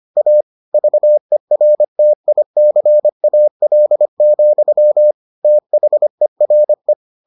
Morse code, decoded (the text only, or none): A VERTICAL, THERE